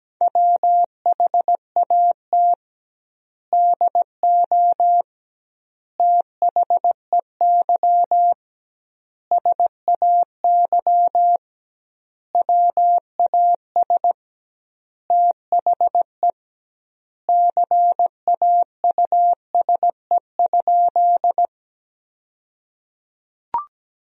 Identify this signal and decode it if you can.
Morse code, heard as a keyed tone, about 17 wpm, 700 Hz. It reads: WHAT DO THEY SAY WAS THE CAUSE?
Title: 2x Word-Spacing - Sentences from Top 200 Words - 17wpm